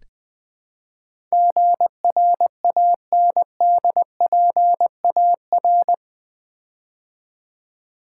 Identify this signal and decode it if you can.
Morse code, heard as a keyed tone, about 20 wpm, 700 Hz. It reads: GRANDPAR
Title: Build A Word - 4 Letters At A Time - 11-Letter Words - 20wpm